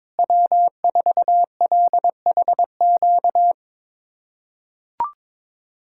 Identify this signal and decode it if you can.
Morse code, heard as a keyed tone, about 22 wpm, 700 Hz. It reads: W4LHQ
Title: USA Callsign DE Callsign - 5 Characters - 22wpm